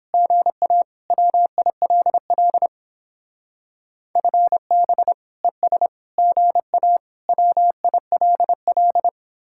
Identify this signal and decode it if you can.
Morse code, heard as a keyed tone, about 31 wpm, 700 Hz. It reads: GA WILL FB ES GA WILL